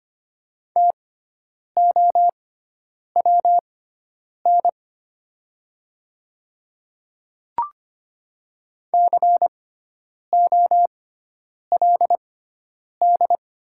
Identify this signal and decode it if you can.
Morse code, heard as a keyed tone, about 25 wpm, spaced as for 9 wpm, 700 Hz. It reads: TOWN COLD
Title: 6x Character Spacing - Top 500 Words - 25wpm